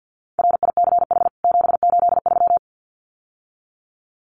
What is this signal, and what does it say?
Morse code, heard as a keyed tone, about 51 wpm, 700 Hz. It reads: RICH 783
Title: CWT Exchange 50wpm